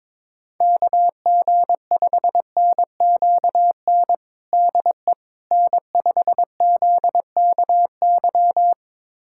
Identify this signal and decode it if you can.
Morse code, heard as a keyed tone, about 22 wpm, 700 Hz. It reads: KG5NQN DE N5ZKY
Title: USA Callsign DE Callsign 22wpm